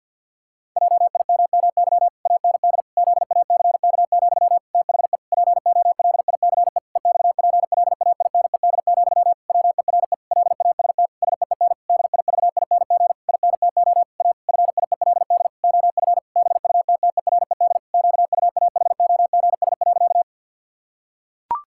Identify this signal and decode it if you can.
Morse code, heard as a keyed tone, about 50 wpm, 700 Hz. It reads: JIMMY AND ZACK, THE POLICE EXPLAINED, WERE LAST SEEN DIVING INTO A FIELD OF BUTTERED QUAHOGS.
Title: Pangram Challenge v2 - 50wpm